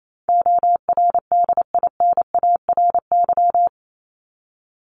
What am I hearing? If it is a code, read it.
Morse code, heard as a keyed tone, about 28 wpm, 700 Hz. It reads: ORDINARY
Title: Top 1751-2000 Words 28wpm